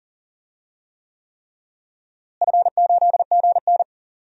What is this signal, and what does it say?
Morse code, heard as a keyed tone, about 39 wpm, 700 Hz. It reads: F8GN
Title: DXCC-100 Call Signs - 4 Characters - Repeated 3x - 40wpm